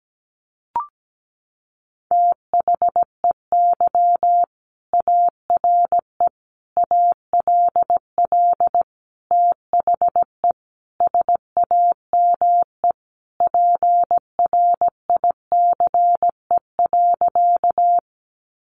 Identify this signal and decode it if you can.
Morse code, heard as a keyed tone, about 17 wpm, 700 Hz. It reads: THEY ARE ALL THE SAME PRICE.
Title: Sentences from Top 600 Words 17wpm